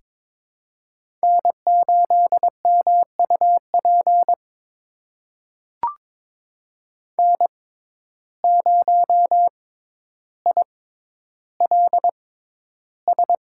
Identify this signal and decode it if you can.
Morse code, heard as a keyed tone, about 22 wpm, spaced as for 11 wpm, 700 Hz. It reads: N8MUP N0ILS
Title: USA Call Signs - 5 Characters - 6x Character Spacing - 22wpm